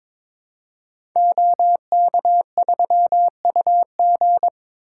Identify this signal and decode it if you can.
Morse code, heard as a keyed tone, about 22 wpm, 700 Hz. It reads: OK3UG
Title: Code Groups - 5 Letter:Numbers - 22wpm